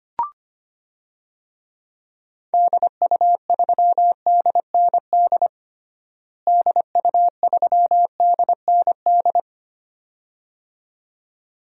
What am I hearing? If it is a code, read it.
Morse code, heard as a keyed tone, about 25 wpm, 700 Hz. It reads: DU3DND DU3DND
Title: DXCC-100 Call Signs - 6 Characters - Repeated 2x - 25wpm